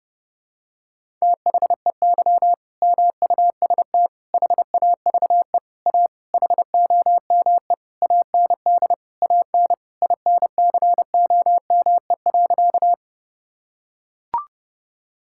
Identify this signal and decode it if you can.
Morse code, heard as a keyed tone, about 30 wpm, 700 Hz. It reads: THEY MUST HAVE A HOME AND AN INCOME.
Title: Sentences from Top 1000 Words - Review Part 5 - 30wpm